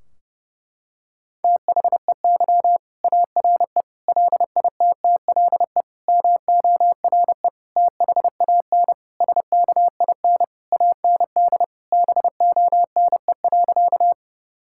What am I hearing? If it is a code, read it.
Morse code, heard as a keyed tone, about 30 wpm, 700 Hz. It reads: THEY ARE LITTLE MORE THAN SKIN AND BONE.
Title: Sentences from Top 1600 Words 30wpm